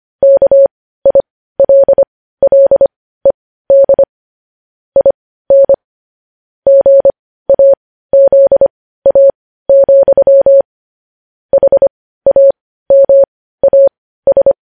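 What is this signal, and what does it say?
Morse code, heard as a keyed tone, about 25 wpm, spaced as for 15 wpm, 550 Hz. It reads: KILLED IN GAZA, HAMAS
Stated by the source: Morse code news headlines at 15.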